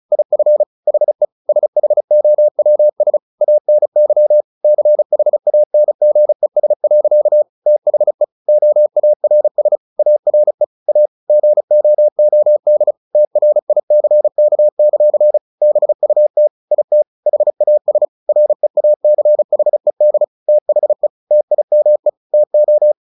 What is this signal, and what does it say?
Morse code, heard as a keyed tone, about 34 wpm, 600 Hz. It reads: IF HE SHOWS ANY CHANGES. THE OARS ARE A GOOD TRICK; BUT IT HAS REACHED THE TIME TO
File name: the old man and the sea 2 35wpm